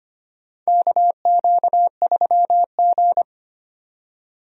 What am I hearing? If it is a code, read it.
Morse code, heard as a keyed tone, about 25 wpm, 700 Hz. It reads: KQ3G
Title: USA Call Signs - 4 Characters - 25wpm